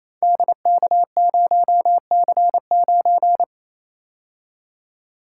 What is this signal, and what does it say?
Morse code, heard as a keyed tone, about 28 wpm, 700 Hz. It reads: DK0C9
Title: Code Groups - 5 Letter:Numbers - 28wpm